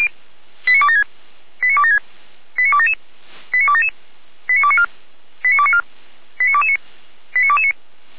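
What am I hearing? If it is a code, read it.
Five-tone selective call (ZVEI), heard as quick runs of five tones, tones of about 70 ms. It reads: 89276 89276 89277 89277 89294 89294 89229 89229
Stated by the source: ZVEI five-tone selective call received on an SDR.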